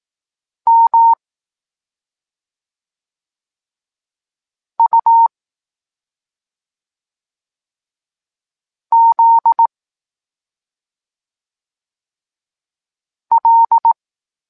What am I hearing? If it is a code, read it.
Morse code, heard as a keyed tone, about 18 wpm, 920 Hz. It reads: M U Z L
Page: Tekst 23.3 – Ciąg dalszy doskonalenia odbioru poznanych liter